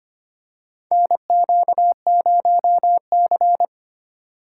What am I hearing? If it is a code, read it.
Morse code, heard as a keyed tone, about 25 wpm, 700 Hz. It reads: NQ0C
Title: USA Call Signs - 4 Characters - 3x Character Spacing - 25wpm